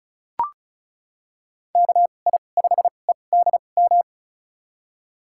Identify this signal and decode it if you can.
Morse code, heard as a keyed tone, about 36 wpm, spaced as for 25 wpm, 700 Hz. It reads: KI5EDM
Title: USA Call Signs - 6 Characters - 2x Character Spacing - 35wpm